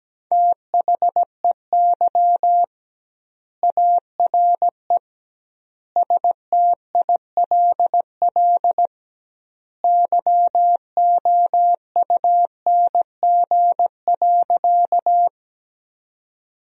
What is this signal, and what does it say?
Morse code, heard as a keyed tone, about 17 wpm, 700 Hz. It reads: THEY ARE STILL YOUNG.
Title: 2x Word-Spacing - Sentences from Top 400 Words - 17wpm